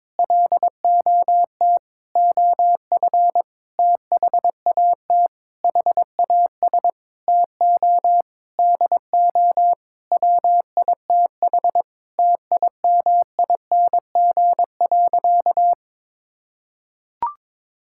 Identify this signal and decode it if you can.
Morse code, heard as a keyed tone, about 22 wpm, 700 Hz. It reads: LOT OF THAT HAS TO DO WITH TIMING.
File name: practice / 22wpm / Sentences with 8-Word Phrases - 22wpm